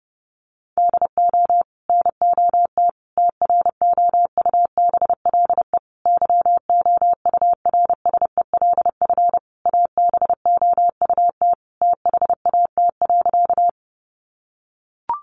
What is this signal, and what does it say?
Morse code, heard as a keyed tone, about 30 wpm, 700 Hz. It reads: DO NOT TROUBLE YOURSELF ABOUT THAT.
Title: Sentences from Top 1200 Words 30wpm